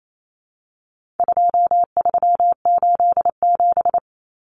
Morse code, heard as a keyed tone, about 28 wpm, 700 Hz. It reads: 2387
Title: Common Cut Numbers - 4-Digit 28wpm